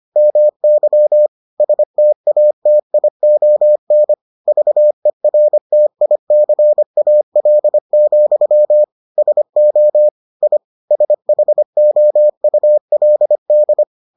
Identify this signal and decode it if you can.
Morse code, heard as a keyed tone, about 25 wpm, 600 Hz. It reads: MY STATION VERTICAL, SO I SHOULD